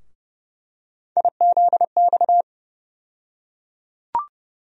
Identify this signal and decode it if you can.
Morse code, heard as a keyed tone, about 30 wpm, 700 Hz. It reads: IZX
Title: USA Call Sign Postfixes - 3 Letters 30wpm